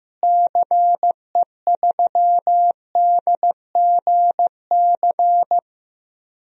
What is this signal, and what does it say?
Morse code, heard as a keyed tone, about 15 wpm, 700 Hz. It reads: CE3DGC
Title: DXCC-100 Call Signs - 6 Characters - Repeated 2x - 15wpm